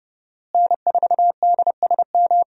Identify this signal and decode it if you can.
Morse code, heard as a keyed tone, about 30 wpm, 700 Hz. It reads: N4DSM
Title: USA Call Signs - 5 Characters - Repeated 3x - 30wpm